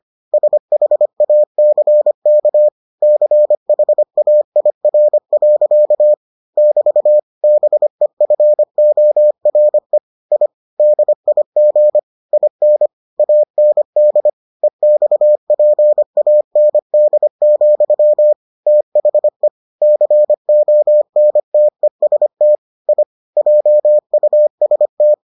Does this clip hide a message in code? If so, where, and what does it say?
Yes: Morse code from 0.3 to 25.1 s, heard as a keyed tone, about 25 wpm, 600 Hz. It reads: SHACK CHAIR. <BT> BEFORE I DIG IN AND EXPAND, THE CONTEST I JUST